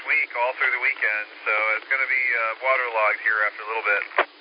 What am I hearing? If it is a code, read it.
Emergency Alert System (SAME) tones, heard as harsh buzzing bursts, with speech. It is heard from 0.5 to 3.6 s.